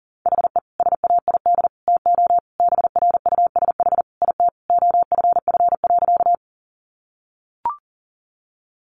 Morse code, heard as a keyed tone, about 40 wpm, 700 Hz. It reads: HE SAID TO BRUSH IT OFF.
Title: Sentences from Top 1100 Words 40wpm